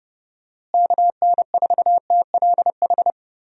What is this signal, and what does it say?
Morse code, heard as a keyed tone, about 31 wpm, 700 Hz. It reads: KN4TLH
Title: USA Call Signs - 6 Characters - Repeated 3x - 30wpm